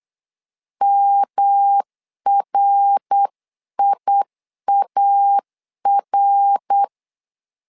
Morse code, heard as a keyed tone, about 8 wpm, 800 Hz. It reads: MRIAR
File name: LEZIONI / 3 MWRO / 3 MWR 8wpm 2444